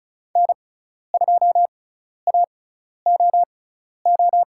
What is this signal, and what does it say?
Morse code, heard as a keyed tone, about 34 wpm, spaced as for 12 wpm, 700 Hz. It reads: N2AOO
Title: USA Call Signs - 6x Character Spacing - 35wpm